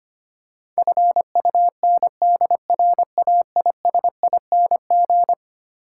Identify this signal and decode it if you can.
Morse code, heard as a keyed tone, about 25 wpm, 700 Hz. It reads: FUNDRAISING
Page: morse-code-ninja-practice-25wpm